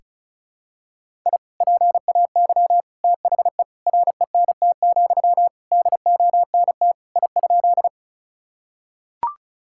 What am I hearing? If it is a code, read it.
Morse code, heard as a keyed tone, about 34 wpm, 700 Hz. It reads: I PAY THE RENT, DONT I?
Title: Sentences from Top 1000 Words - Review Part 5 - 35wpm